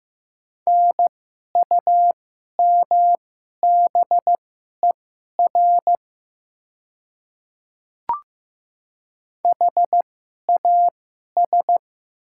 Morse code, heard as a keyed tone, about 15 wpm, spaced as for 11 wpm, 700 Hz. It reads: NUMBER HAS
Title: 2x Character Spacing - Top 100 Words - 15wpm